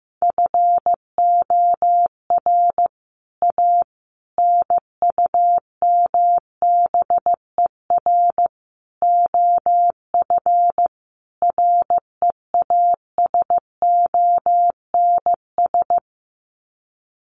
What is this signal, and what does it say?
Morse code, heard as a keyed tone, about 15 wpm, 700 Hz. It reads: FOR A NUMBER OF REASONS
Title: Sets of 5 Words Encore2 15wpm